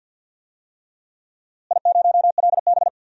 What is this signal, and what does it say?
Morse code, heard as a keyed tone, about 53 wpm, 700 Hz. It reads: I0LB
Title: DXCC-100 Call Signs - 4 Characters - Repeated 3x - 50wpm